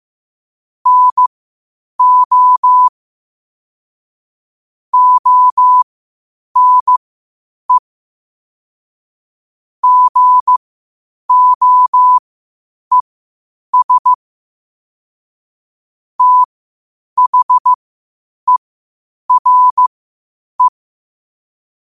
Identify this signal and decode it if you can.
Morse code, heard as a keyed tone, about 15 wpm, spaced as for 8 wpm, 1000 Hz. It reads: NO ONE GOES THERE
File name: qotdfiles / 8wpm